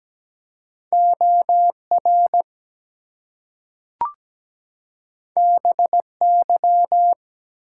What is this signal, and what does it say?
Morse code, heard as a keyed tone, about 17 wpm, 700 Hz. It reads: OR BY
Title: Two Letter Words - 17wpm